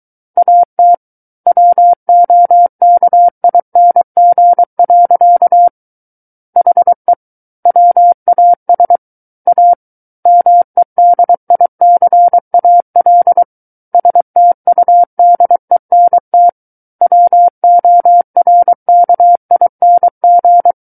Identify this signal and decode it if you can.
Morse code, heard as a keyed tone, about 23 wpm, 700 Hz. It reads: AT WOKING. HE WAS A MEDICAL STUDENT WORKING
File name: Book One / 14-Chapter 14 - 23 WPM